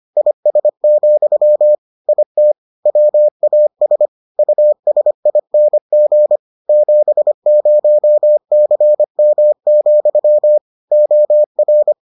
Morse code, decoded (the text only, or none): IS, IT WAS USING 70CM, OR